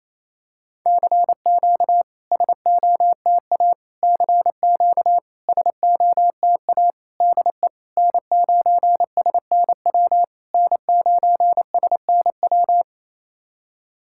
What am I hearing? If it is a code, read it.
Morse code, heard as a keyed tone, about 28 wpm, 660 Hz. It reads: CQ SOTA CQ SOTA DE N9SNW N9SNW
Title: SOTA QSO Practice 28wpm